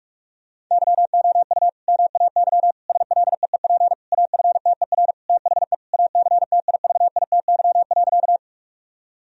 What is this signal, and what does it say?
Morse code, heard as a keyed tone, about 46 wpm, 700 Hz. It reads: YOU MAY SLEEP AFTER THE ACTIVITY.
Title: Sentences from Top 1000 Words - Review Part 4 - 45wpm